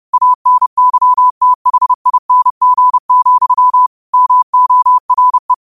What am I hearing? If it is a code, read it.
Morse code, heard as a keyed tone, about 30 wpm, 1000 Hz. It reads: ANYTHING, MORE